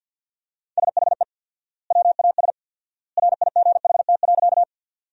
Morse code, heard as a keyed tone, about 50 wpm, 700 Hz. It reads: SHE WAS RIGHT.
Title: 4x Word-Spacing - Sentences from Top 200 Words - 50wpm